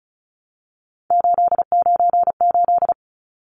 Morse code, heard as a keyed tone, about 34 wpm, 700 Hz. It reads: 898
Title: Numbers - 3-Digit 35wpm